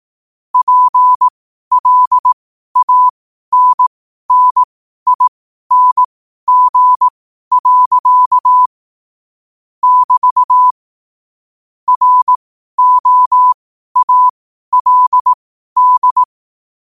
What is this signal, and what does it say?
Morse code, heard as a keyed tone, about 18 wpm, spaced as for 12 wpm, 1000 Hz. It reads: PLANNING. <BT> ROALD
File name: qotdfiles / 12wpm